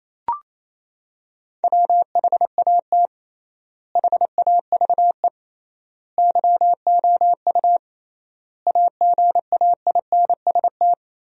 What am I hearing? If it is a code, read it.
Morse code, heard as a keyed tone, about 28 wpm, 700 Hz. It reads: WHAT HAVE YOU AGAINST
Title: 3x Word-Spacing - Sentences from Top 400 Words - 28wpm